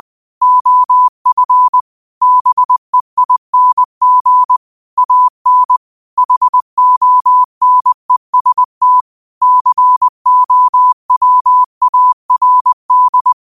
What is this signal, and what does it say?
Morse code, heard as a keyed tone, about 20 wpm, 1000 Hz. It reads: OF BEING AN HONEST COWARD